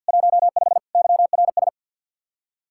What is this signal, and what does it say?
Morse code, heard as a keyed tone, about 63 wpm, 700 Hz. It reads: 15 YRS